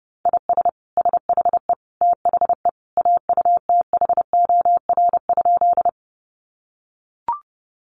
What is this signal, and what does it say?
Morse code, heard as a keyed tone, about 30 wpm, 700 Hz. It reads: IS SHE THE AUTHOR?
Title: Sentences from Top 1200 Words 30wpm